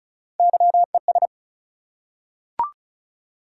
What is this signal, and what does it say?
Morse code, heard as a keyed tone, about 36 wpm, 700 Hz. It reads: YES
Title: Three Letter Words - 35wpm